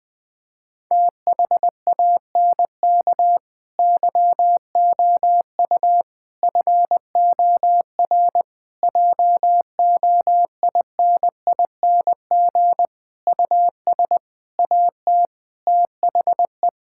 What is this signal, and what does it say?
Morse code, heard as a keyed tone, about 21 wpm, 700 Hz. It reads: THANK YOU FOR JOINING US AT THE
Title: Sentences with 5-Word Phrases - 20wpm